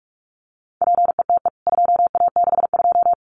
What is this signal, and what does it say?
Morse code, heard as a keyed tone, about 44 wpm, 700 Hz. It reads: PETE 2A62
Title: CWT Exchange 45wpm